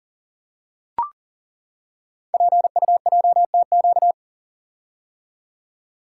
Morse code, heard as a keyed tone, about 41 wpm, 700 Hz. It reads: PUJTQ
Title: Code Groups - 5 Letter:Numbers - 40wpm